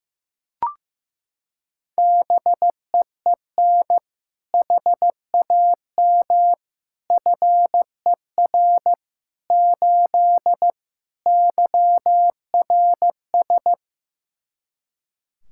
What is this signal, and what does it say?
Morse code, heard as a keyed tone, about 16 wpm, 700 Hz. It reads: BEEN HAM FER 8 YRS
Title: Practice QSO - Exchange 3 - More Info - 15wpm